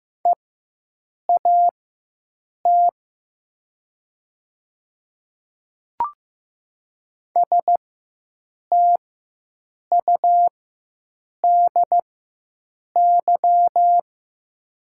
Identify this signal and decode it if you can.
Morse code, heard as a keyed tone, about 15 wpm, spaced as for 7 wpm, 700 Hz. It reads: EAT STUDY
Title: 4x Character Spacing - Top 300 Words - 15wpm